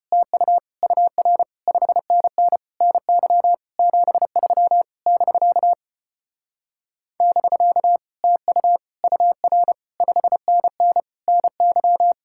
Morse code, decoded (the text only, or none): TU UR 5NN NY 73 <BK> <BK> TU UR 5NN NY